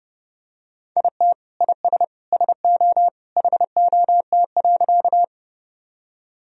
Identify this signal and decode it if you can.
Morse code, heard as a keyed tone, about 32 wpm, 700 Hz. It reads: IT IS SO HOT.
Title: Sentences from Top 1000 Words - Review Part 3 - 30wpm